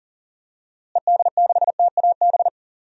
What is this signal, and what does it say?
Morse code, heard as a keyed tone, about 39 wpm, 700 Hz. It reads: ED6TAB
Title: DXCC-100 Call Signs - 6 Characters - Repeated 3x - 40wpm